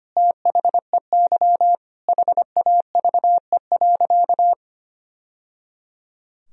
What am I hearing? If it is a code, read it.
Morse code, heard as a keyed tone, about 25 wpm, 700 Hz. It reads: THEY HAVE.
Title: Sentences from Top 1000 Words - Review Part 2 - 25wpm